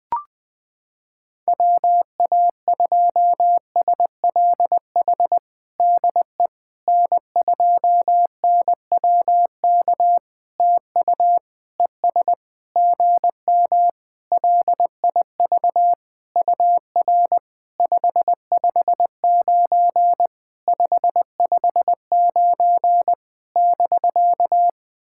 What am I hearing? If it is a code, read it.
Morse code, heard as a keyed tone, about 20 wpm, 700 Hz. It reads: WA2SLH DE N2NWK TU ES GM LIV UR 559 559 <BK>
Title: POTA QSO Practice - 20wpm